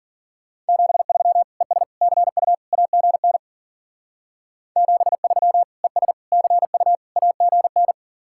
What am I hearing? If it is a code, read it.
Morse code, heard as a keyed tone, about 48 wpm, 700 Hz. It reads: 73 ES CU AGN 73 ES CU AGN
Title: Speed-Racing Common QSO Phrases - 40wpm